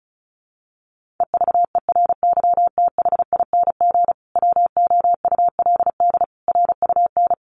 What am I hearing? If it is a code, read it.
Morse code, heard as a keyed tone, about 34 wpm, 700 Hz. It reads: EVERYTHING WOULD RUN
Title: Sentences from Top 1400 Words 35wpm